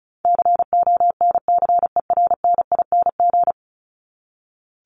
Morse code, heard as a keyed tone, about 34 wpm, 700 Hz. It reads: CONCERNING
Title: Top 1751-2000 Words 35wpm